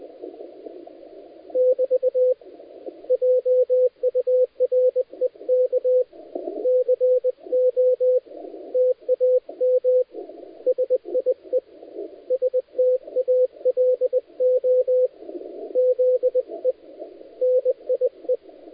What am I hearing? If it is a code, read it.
Morse code, heard as a keyed tone, about 20 wpm, 510 Hz, with radio noise. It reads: <BT> JUREK CO TAM SIE STALO ZE NIE
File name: dlazaawansowanych / nagraniazpasma / titawaNETcontrol